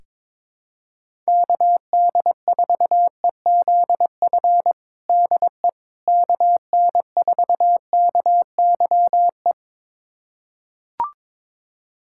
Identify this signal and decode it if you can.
Morse code, heard as a keyed tone, about 22 wpm, 700 Hz. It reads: KD4EZF DE KN4KYE
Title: USA Callsign DE Callsign 22wpm